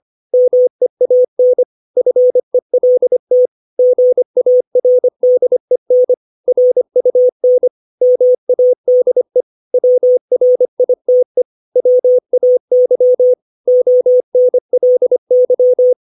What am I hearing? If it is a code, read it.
Morse code, heard as a keyed tone, about 25 wpm, 500 Hz. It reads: MEAN FELT GARDEN RUN MADE WRITE WAY ONLY
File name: 632 words / 632 Words 25WPM